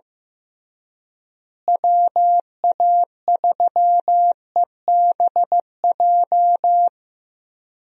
Morse code, heard as a keyed tone, about 15 wpm, 700 Hz. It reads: WA3EBJ
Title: USA Call Signs - 6 Characters - Repeated 3x - 15wpm